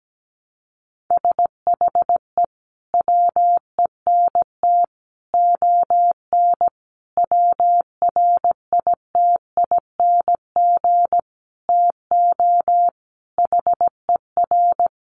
Morse code, heard as a keyed tone, about 17 wpm, 700 Hz. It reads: SHE WENT ON WRITING TO HER
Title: Sentences from Top 800 Words 17wpm